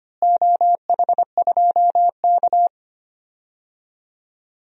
Morse code, heard as a keyed tone, about 25 wpm, 700 Hz. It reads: OH2K
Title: DXCC-100 Call Signs - 4 Characters - Repeated 2x - 25wpm